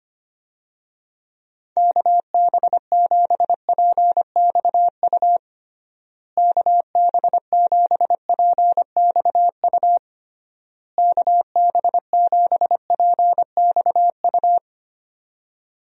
Morse code, heard as a keyed tone, about 25 wpm, 700 Hz. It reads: KB7PXU KB7PXU KB7PXU
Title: USA Call Signs - 6 Characters - Repeated 3x - 25wpm